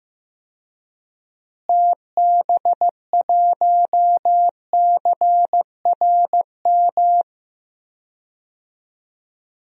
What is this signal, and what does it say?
Morse code, heard as a keyed tone, about 15 wpm, 700 Hz. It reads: TB1CRM
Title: DXCC-100 Call Signs - 6 Characters - Repeated 3x - 15wpm